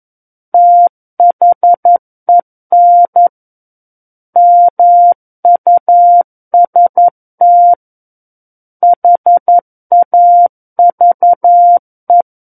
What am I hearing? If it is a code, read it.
Morse code, heard as a keyed tone, about 11 wpm, 700 Hz. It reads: THEN MUST HAVE
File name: Book One / 02-Chapter 2 - 11 WPM